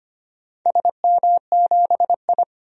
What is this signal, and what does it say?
Morse code, heard as a keyed tone, about 25 wpm, 700 Hz. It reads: SM7I
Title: DXCC-100 Call Signs - 4 Characters - 25wpm